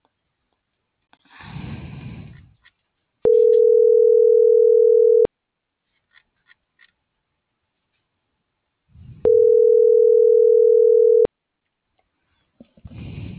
Ringback tone (North America), 440 + 480 Hz.